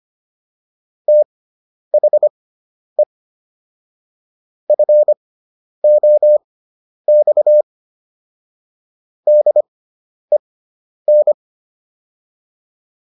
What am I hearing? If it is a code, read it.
Morse code, heard as a keyed tone, about 25 wpm, spaced as for 10 wpm, 600 Hz. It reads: THE FOX DEN